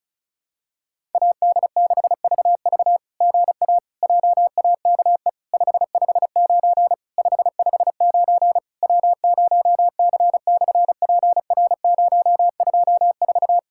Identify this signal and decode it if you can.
Morse code, heard as a keyed tone, about 36 wpm, 700 Hz. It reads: AD6VV GA JAKE 559 559 W0C/PR024
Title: SOTA QSO Practice 35wpm